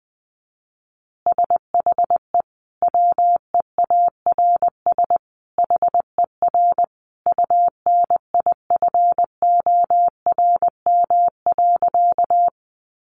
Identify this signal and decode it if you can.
Morse code, heard as a keyed tone, about 20 wpm, 700 Hz. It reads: SHE WEARS HER UNIFORM.
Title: Easy Phrases 20wpm